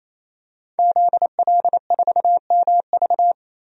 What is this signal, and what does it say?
Morse code, heard as a keyed tone, about 28 wpm, 700 Hz. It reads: ZL4MV
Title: DXCC-100 Call Signs - 5 Characters - 28wpm